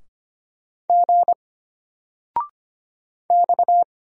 Morse code, heard as a keyed tone, about 25 wpm, 700 Hz. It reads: G X
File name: Single-Letter-Number-25wpm